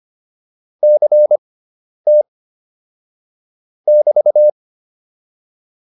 Morse code, heard as a keyed tone, about 25 wpm, spaced as for 10 wpm, 600 Hz. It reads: CT <BT>